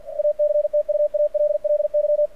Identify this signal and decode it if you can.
Morse code, heard as a keyed tone, about 50 wpm, 600 Hz. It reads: ACTUALLY
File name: archived / CRSnet